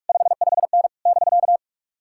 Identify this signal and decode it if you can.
Morse code, heard as a keyed tone, about 57 wpm, 700 Hz. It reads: 55N <BK>